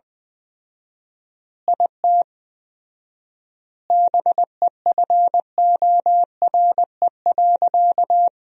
Morse code, heard as a keyed tone, about 20 wpm, 700 Hz. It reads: IT BEFORE.
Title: 4x Word-Spacing - Sentences from Top 400 Words - 20wpm